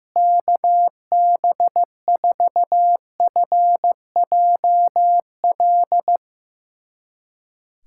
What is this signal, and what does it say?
Morse code, heard as a keyed tone, about 15 wpm, 700 Hz. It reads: KB4FJL
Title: USA Callsign DE Callsign - 6 Characters - 15wpm